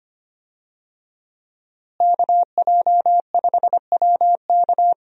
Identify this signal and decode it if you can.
Morse code, heard as a keyed tone, about 25 wpm, 700 Hz. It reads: KJ5WK
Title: USA Call Signs - 5 Characters - Repeated 2x - 25wpm